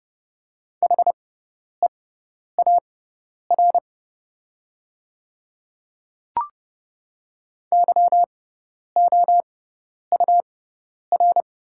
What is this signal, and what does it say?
Morse code, heard as a keyed tone, about 29 wpm, spaced as for 10 wpm, 700 Hz. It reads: HEAR YOUR
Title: 6x Character Spacing - Top 300 Words - 30wpm